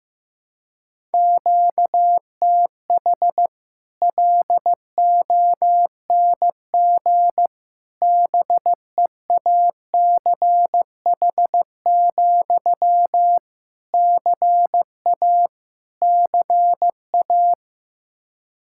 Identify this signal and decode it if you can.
Morse code, heard as a keyed tone, about 15 wpm, 700 Hz. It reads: QTH LONG BEACH, CA CA
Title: Practice QSO - Exchange 1 - Beginning - 15wpm